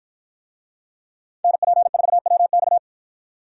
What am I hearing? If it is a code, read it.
Morse code, heard as a keyed tone, about 49 wpm, 700 Hz. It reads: NP4WX